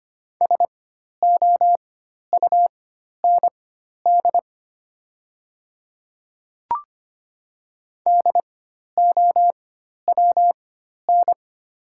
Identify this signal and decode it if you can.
Morse code, heard as a keyed tone, about 25 wpm, spaced as for 12 wpm, 700 Hz. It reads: SOUND DOWN
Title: 4x Character Spacing - Top 100 Words - 25wpm